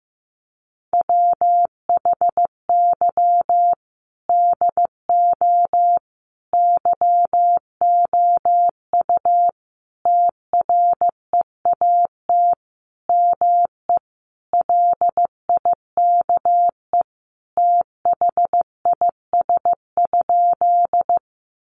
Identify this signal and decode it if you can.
Morse code, heard as a keyed tone, about 15 wpm, 700 Hz. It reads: WHY DO YOU TREAT ME LIKE THIS?
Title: Sentences from Top 1300 Words 15wpm